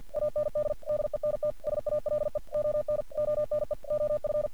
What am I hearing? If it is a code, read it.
Morse code, heard as a keyed tone, about 48 wpm, 600 Hz. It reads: AND DENT SALE ON ONE OF
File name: archived / CRSnet